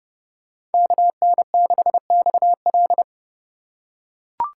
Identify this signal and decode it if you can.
Morse code, heard as a keyed tone, about 30 wpm, 690 Hz. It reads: KN6XL
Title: USA Call Signs - 5 Characters - 2x Character Spacing - 30wpm